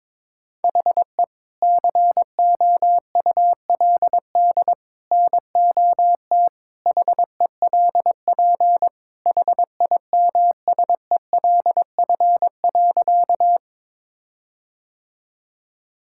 Morse code, heard as a keyed tone, about 21 wpm, 700 Hz. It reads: HE COULD NOT HELP HIMSELF.
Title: Sentences from Top 1600 Words 22wpm